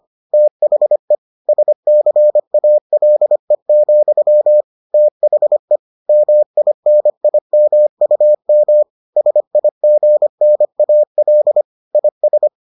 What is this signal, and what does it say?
Morse code, heard as a keyed tone, about 25 wpm, 600 Hz. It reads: THE SCALE, THE MINIMUM SIGNAL IS